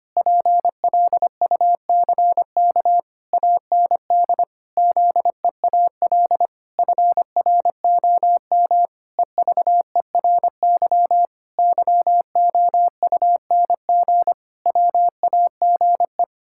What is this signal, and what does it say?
Morse code, heard as a keyed tone, about 25 wpm, 700 Hz. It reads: PLUCK AND ZEAL FROM EVERY YOUNG WAGE
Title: Pangram Challenge v2 - 25wpm